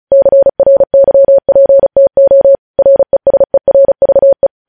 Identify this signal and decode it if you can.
Morse code, heard as a keyed tone, about 36 wpm, 550 Hz. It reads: CRYPTO RESERVE
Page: News Headlines in Morse Code at 35 WPM